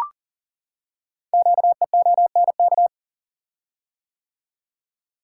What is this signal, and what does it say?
Morse code, heard as a keyed tone, about 40 wpm, 700 Hz. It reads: QEONK
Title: Code Groups - 5 Letters - 40wpm